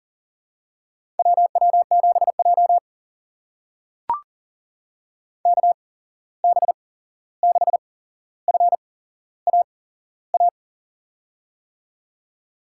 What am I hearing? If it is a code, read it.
Morse code, heard as a keyed tone, about 40 wpm, spaced as for 15 wpm, 700 Hz. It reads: WW7J KB6FAA
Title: USA Call Signs - 8x Character Spacing - 40wpm